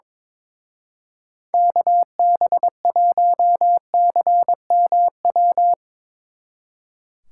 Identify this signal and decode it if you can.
Morse code, heard as a keyed tone, about 23 wpm, 700 Hz. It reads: KB1CMW
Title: Speed Racing - USA Call Signs 22wpm